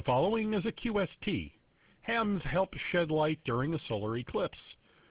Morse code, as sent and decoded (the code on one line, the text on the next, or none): none